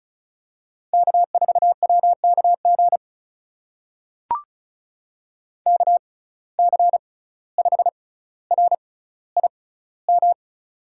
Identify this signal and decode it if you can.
Morse code, heard as a keyed tone, about 36 wpm, spaced as for 17 wpm, 700 Hz. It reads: K4WKG KC5RIM
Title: USA Call Signs - 6x Character Spacing - 35wpm